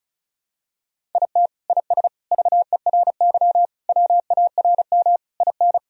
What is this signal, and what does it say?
Morse code, heard as a keyed tone, about 36 wpm, 700 Hz. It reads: IT IS VERY WARM IN
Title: Sentences from Top 1000 Words - Review Part 2 - 35wpm